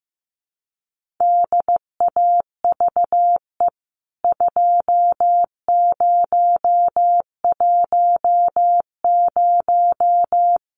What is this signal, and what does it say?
Morse code, heard as a keyed tone, about 15 wpm, 700 Hz. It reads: DAVE 2010
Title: CWT Exchange 15wpm